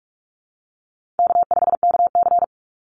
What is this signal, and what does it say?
Morse code, heard as a keyed tone, about 44 wpm, 700 Hz. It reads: K5KC
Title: Easy Call Signs Repeated 2x - 45wpm